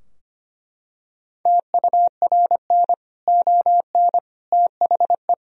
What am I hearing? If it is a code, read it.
Morse code, heard as a keyed tone, about 25 wpm, 700 Hz. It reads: TURN ON THE
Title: Sentences from Top 1000 Words - Review Part 3 - 25wpm